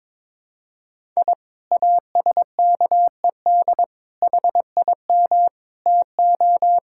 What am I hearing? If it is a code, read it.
Morse code, heard as a keyed tone, about 22 wpm, 700 Hz. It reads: I ASKED HIM TO
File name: practice / 22wpm / Sentences with 2-Word Phrases - 22wpm